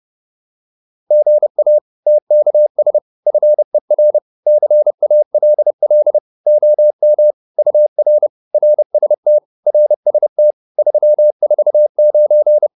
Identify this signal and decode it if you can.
Morse code, heard as a keyed tone, about 30 wpm, 600 Hz. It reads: GA TKS FER CALL OM UR RST RST 349